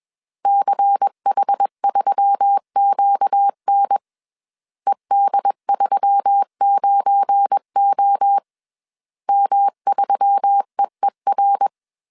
Morse code, heard as a keyed tone, about 21 wpm, 800 Hz. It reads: CH3QN ED39O M3EER